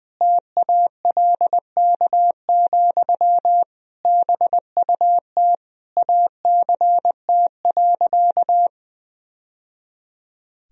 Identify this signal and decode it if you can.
Morse code, heard as a keyed tone, about 19 wpm, 700 Hz. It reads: TALK, BUT ACT.
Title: Sentences from Top 1000 Words - Review Part 1 - 20wpm